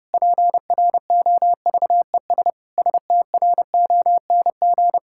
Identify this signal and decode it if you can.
Morse code, heard as a keyed tone, about 30 wpm, 700 Hz. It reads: PROVES STRONG
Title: Pangram Challenge v2 - 30wpm